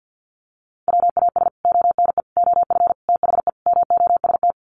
Morse code, heard as a keyed tone, about 50 wpm, 700 Hz. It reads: WAS ONE OF THE MOST
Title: Sets of 5 Words 50wpm